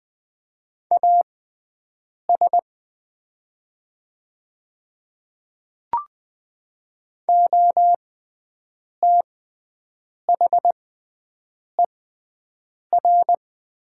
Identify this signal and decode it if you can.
Morse code, heard as a keyed tone, about 20 wpm, spaced as for 7 wpm, 700 Hz. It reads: AS OTHER
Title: 6x Character Spacing - Top 100 Words - 20wpm